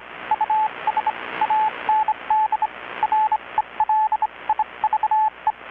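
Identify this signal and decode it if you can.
Morse code, heard as a keyed tone, about 25 wpm, spaced as for 22 wpm, 860 Hz, with radio noise. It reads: US AND RELIVE
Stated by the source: Morse from KPH's Night of Nights XX maritime radiotelegraphy event, received in Chiapas, Mexico about 3,800 km away on a Kenwood R-600 with a grounded 10 m long-wire antenna.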